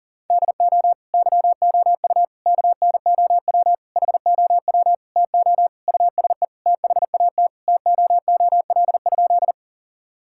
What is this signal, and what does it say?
Morse code, heard as a keyed tone, about 40 wpm, 700 Hz. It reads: DO YOU KNOW HOW TO USE THAT TOOL?
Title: Sentences from Top 1000 Words - Review Part 6 - 40wpm